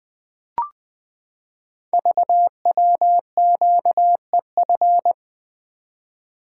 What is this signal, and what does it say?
Morse code, heard as a keyed tone, about 20 wpm, 700 Hz. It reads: VWQEF
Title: Code Groups - 5 Letters - 20wpm